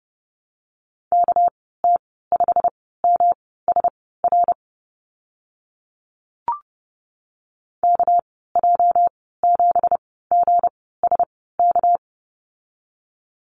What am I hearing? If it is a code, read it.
Morse code, heard as a keyed tone, about 30 wpm, spaced as for 17 wpm, 700 Hz. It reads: KT5MSR KJ7GSK
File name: Lesson 37g - Letter x - Call Signs 30wpm